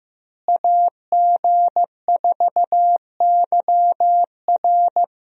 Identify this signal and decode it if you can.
Morse code, heard as a keyed tone, about 15 wpm, 700 Hz. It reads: AG4YR
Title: USA Call Signs - 5 Characters - Repeated 2x - 15wpm